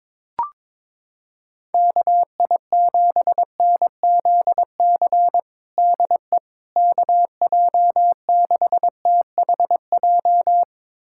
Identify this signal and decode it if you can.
Morse code, heard as a keyed tone, about 22 wpm, 700 Hz. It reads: KI7NZC DE KJ6THJ
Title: USA Callsign DE Callsign - 6 Characters - 22wpm